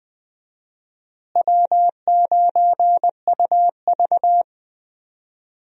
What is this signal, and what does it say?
Morse code, heard as a keyed tone, about 20 wpm, 700 Hz. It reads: W9UV
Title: USA Call Signs - 4 Characters - Repeated 2x - 20wpm